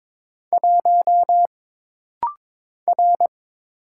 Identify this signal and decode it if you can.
Morse code, heard as a keyed tone, about 22 wpm, 700 Hz. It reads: 1 R
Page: This rapid-fire timing is geared to push your ICR (Instant Character Recognition) skills.